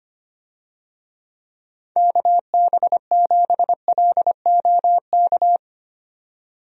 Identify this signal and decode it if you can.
Morse code, heard as a keyed tone, about 25 wpm, 700 Hz. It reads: KB7LOK
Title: USA Call Signs - 6 Characters - 25wpm